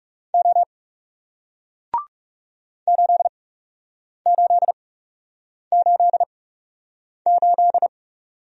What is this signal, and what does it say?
Morse code, heard as a keyed tone, about 41 wpm, spaced as for 8 wpm, 700 Hz. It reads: O 8888